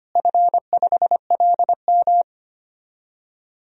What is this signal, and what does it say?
Morse code, heard as a keyed tone, about 25 wpm, 700 Hz. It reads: F5LM
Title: DXCC-100 Call Signs - 4 Characters - Repeated 3x - 25wpm